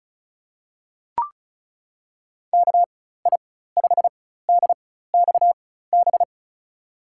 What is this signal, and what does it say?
Morse code, heard as a keyed tone, about 34 wpm, spaced as for 16 wpm, 690 Hz. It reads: KI5DXB
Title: USA Call Signs - 6 Characters - 4x Character Spacing - 35wpm